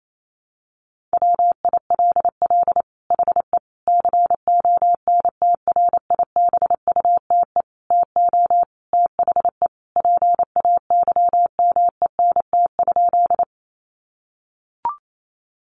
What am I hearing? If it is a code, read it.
Morse code, heard as a keyed tone, about 28 wpm, 700 Hz. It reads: WILL HE CONTRIBUTE TO THE PAYMENT?
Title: Sentences from Top 1400 Words 28wpm